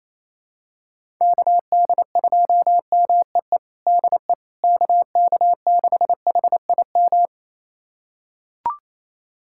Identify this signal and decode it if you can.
Morse code, heard as a keyed tone, about 28 wpm, 700 Hz. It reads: KD2MEE DE KK6HIM
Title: USA Callsign DE Callsign - 6 Characters - 28wpm